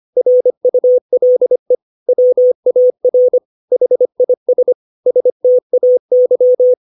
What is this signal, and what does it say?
Morse code, heard as a keyed tone, about 25 wpm, 500 Hz. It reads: RULE WAR HIS STAY
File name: 632 words / 632 Words 25WPM